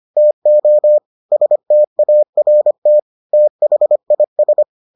Morse code, heard as a keyed tone, about 25 wpm, 600 Hz. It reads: TO START THIS